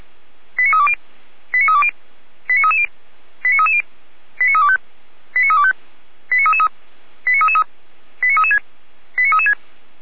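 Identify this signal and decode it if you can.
Five-tone selective call (ZVEI), heard as quick runs of five tones, tones of about 70 ms. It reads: 89320 89320 89330 89330 89325 89325 89333 89333 89336 89336
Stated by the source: ZVEI five-tone selective call 89315 received via SDR, Feuerwehr Orlamünde fire brigade.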